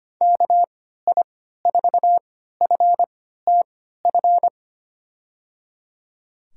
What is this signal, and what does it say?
Morse code, heard as a keyed tone, about 25 wpm, spaced as for 14 wpm, 700 Hz. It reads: KI4FTF
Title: USA Call Signs - 6 Characters - 3x Character Spacing - 25wpm